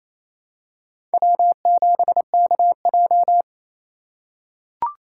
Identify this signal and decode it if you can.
Morse code, heard as a keyed tone, about 28 wpm, 700 Hz. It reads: W7KJ
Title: USA Call Signs - 3x Character Spacing - 28wpm